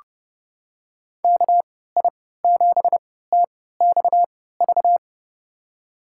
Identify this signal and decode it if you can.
Morse code, heard as a keyed tone, about 30 wpm, spaced as for 17 wpm, 700 Hz. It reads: KI7TXV
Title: USA Call Signs - 6 Characters - 3x Character Spacing - 30wpm